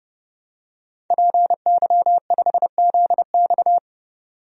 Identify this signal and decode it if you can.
Morse code, heard as a keyed tone, about 30 wpm, 700 Hz. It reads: PY5ZX
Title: DXCC-100 Call Signs - 5 Characters - 30wpm